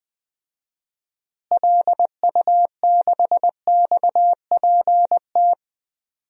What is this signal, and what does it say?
Morse code, heard as a keyed tone, about 21 wpm, 700 Hz. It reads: LU6XPT
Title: DXCC-100 Call Signs - 6 Characters - Repeated 2x - 20wpm